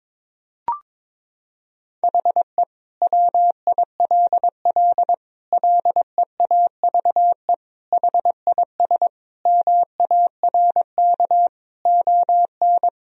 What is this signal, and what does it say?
Morse code, heard as a keyed tone, about 22 wpm, 700 Hz. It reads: HE WILL LEAVE HIS MARK ON
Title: Sentences from Top 1000 Words - Review Part 1 - 22wpm